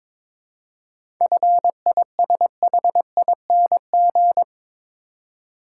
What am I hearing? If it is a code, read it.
Morse code, heard as a keyed tone, about 22 wpm, 700 Hz. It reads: FISHING
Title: Seven Letter Words - 22wpm